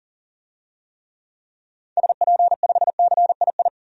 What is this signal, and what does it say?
Morse code, heard as a keyed tone, about 40 wpm, 700 Hz. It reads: SP5CII
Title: DXCC-100 Call Signs - 6 Characters - 40wpm